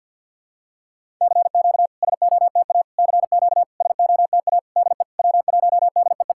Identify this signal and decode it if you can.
Morse code, heard as a keyed tone, about 50 wpm, 700 Hz. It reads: CQ SOTA CQ SOTA DE W1DEE
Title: SOTA QSO Practice 50wpm